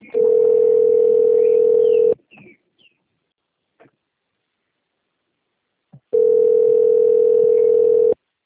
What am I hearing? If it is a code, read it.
Ringback tone (North America), 440 + 480 Hz.
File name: rg-1600-unknown- - - .23527